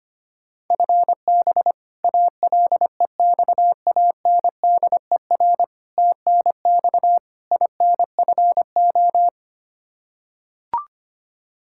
Morse code, heard as a keyed tone, about 25 wpm, 700 Hz. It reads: FB ALEXANDER TNX INFO
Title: K1USN SST QSO Practice 25wpm